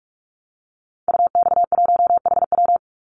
Morse code, heard as a keyed tone, about 44 wpm, 700 Hz. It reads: UX1HW
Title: Call Signs Repeated 2x - 45wpm